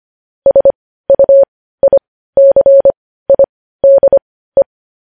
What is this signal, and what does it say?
Morse code, heard as a keyed tone, about 25 wpm, spaced as for 15 wpm, 550 Hz. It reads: SUICIDE
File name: podcast / News headlines-1x-25x15